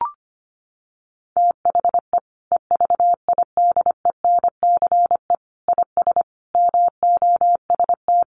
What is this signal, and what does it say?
Morse code, heard as a keyed tone, about 25 wpm, 700 Hz. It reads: THE EVIDENCE IS MOST
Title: Sentences from Top 1300 Words 25wpm